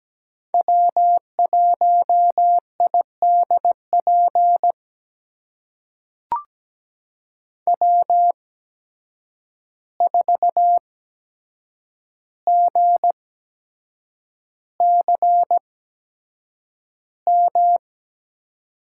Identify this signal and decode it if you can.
Morse code, heard as a keyed tone, about 17 wpm, spaced as for 7 wpm, 700 Hz. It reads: W1IDP W4GCM
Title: USA Call Signs - 5 Characters - 8x Character Spacing - 17wpm